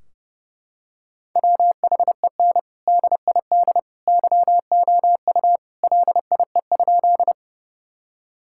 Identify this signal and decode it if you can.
Morse code, heard as a keyed tone, about 30 wpm, 700 Hz. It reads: WHEN DID YOU LIE?
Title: Sentences from Top 1600 Words 30wpm